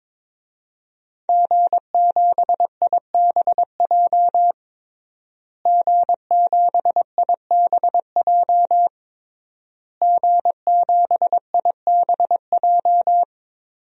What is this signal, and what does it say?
Morse code, heard as a keyed tone, about 22 wpm, 700 Hz. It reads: G7IBJ G7IBJ G7IBJ
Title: DXCC-100 Call Signs - 5 Characters - Repeated 3x - 22wpm